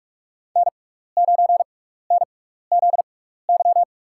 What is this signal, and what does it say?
Morse code, heard as a keyed tone, about 44 wpm, spaced as for 15 wpm, 690 Hz. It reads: N9NZY
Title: USA Call Signs - 5 Characters - 6x Character Spacing - 45wpm